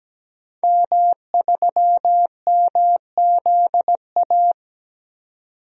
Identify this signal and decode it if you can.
Morse code, heard as a keyed tone, about 17 wpm, 700 Hz. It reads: M3MZA
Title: DXCC-100 Call Signs - 5 Characters - 17wpm